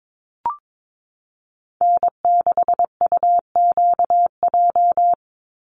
Morse code, heard as a keyed tone, about 22 wpm, 700 Hz. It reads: N6UQJ
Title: USA Call Signs 22wpm